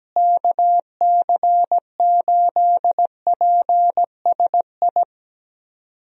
Morse code, heard as a keyed tone, about 17 wpm, 700 Hz. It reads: KC8PSI
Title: USA Call Signs - 6 Characters - Repeated 3x - 17wpm